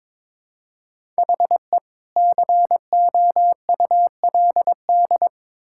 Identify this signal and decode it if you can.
Morse code, heard as a keyed tone, about 21 wpm, 700 Hz. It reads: HE COULD
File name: Build A Phrase - 2-Word Phrases - 22wpm